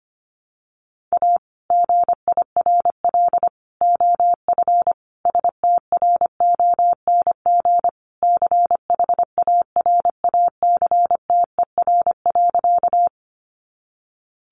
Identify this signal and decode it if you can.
Morse code, heard as a keyed tone, about 25 wpm, 700 Hz. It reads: A GIRL OF STRONG CHARACTER.
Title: Sentences from Top 1200 Words 25wpm